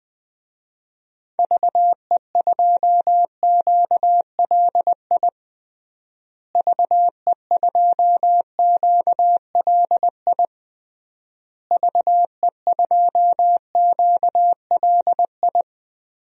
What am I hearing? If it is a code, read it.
Morse code, heard as a keyed tone, about 20 wpm, 700 Hz. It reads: VE2QLI VE2QLI VE2QLI
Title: DXCC-100 Call Signs - 6 Characters - Repeated 3x - 20wpm